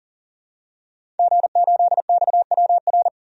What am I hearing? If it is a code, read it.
Morse code, heard as a keyed tone, about 40 wpm, 700 Hz. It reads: G8XWR
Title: DXCC-100 Call Signs - 5 Characters - 40wpm